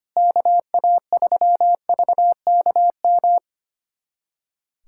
Morse code, heard as a keyed tone, about 25 wpm, 700 Hz. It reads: KA3VKM
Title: USA Callsign DE Callsign - 6 Characters - 25wpm